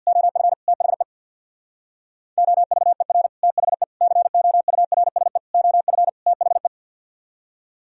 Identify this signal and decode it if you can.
Morse code, heard as a keyed tone, about 56 wpm, 700 Hz. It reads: OF THE OVER THE COURSE OF THE
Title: Speed-Racing - Sets of 5 Words 50wpm